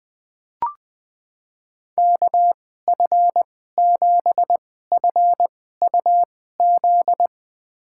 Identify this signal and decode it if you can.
Morse code, heard as a keyed tone, about 20 wpm, spaced as for 15 wpm, 700 Hz. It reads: KF7FUZ
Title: USA Call Signs - 6 Characters - 2x Character Spacing - 20wpm